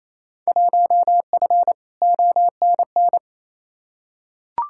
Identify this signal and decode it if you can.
Morse code, heard as a keyed tone, about 28 wpm, 700 Hz. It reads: 1F ONN
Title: Field Day Exchanges Encore - 28wpm